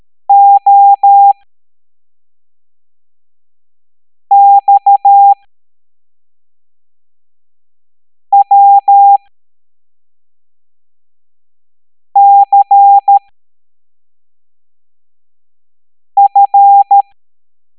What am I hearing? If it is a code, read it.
Morse code, heard as a keyed tone, about 13 wpm, spaced as for 3 wpm, 800 Hz. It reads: OXWCF